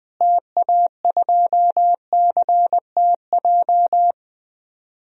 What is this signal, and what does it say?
Morse code, heard as a keyed tone, about 20 wpm, 700 Hz. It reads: TA2CTJ
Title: DXCC-100 Call Signs - 6 Characters - Repeated 2x - 20wpm